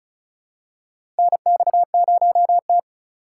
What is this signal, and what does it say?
Morse code, heard as a keyed tone, about 36 wpm, 700 Hz. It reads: NX0T
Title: USA Call Signs - 4 Characters - 3x Character Spacing - 35wpm